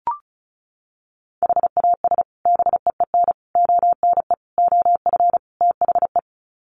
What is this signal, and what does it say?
Morse code, heard as a keyed tone, about 34 wpm, 700 Hz. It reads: HAS BEEN ONE OF THE
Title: Sets of 5 Words Encore 35wpm